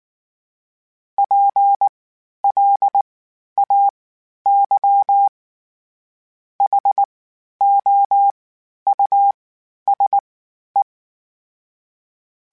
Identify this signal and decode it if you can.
Morse code, heard as a keyed tone, about 19 wpm, 800 Hz. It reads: P L A Y H O U S E
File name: General / General-mp3s / Houses @18